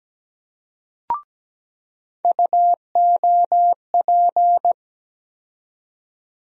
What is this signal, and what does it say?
Morse code, heard as a keyed tone, about 17 wpm, 700 Hz. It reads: UOP
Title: USA Call Sign Postfixes - 3 Letters 17wpm